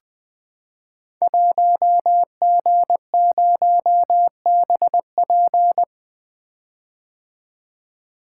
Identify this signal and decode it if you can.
Morse code, heard as a keyed tone, about 20 wpm, 700 Hz. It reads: 1G0BP
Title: Code Groups - 5 Letter:Numbers - 20wpm